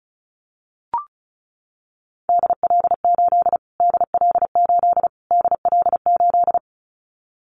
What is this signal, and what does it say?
Morse code, heard as a keyed tone, about 34 wpm, 700 Hz. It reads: DL8 DL8 DL8
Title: Callsign Prefixes for Top 25 DXCC - Repeated 3x - 35wpm